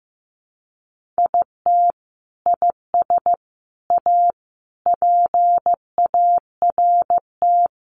Morse code, heard as a keyed tone, about 15 wpm, 700 Hz. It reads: IT IS A PART
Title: Sentences from Top 1400 Words 15wpm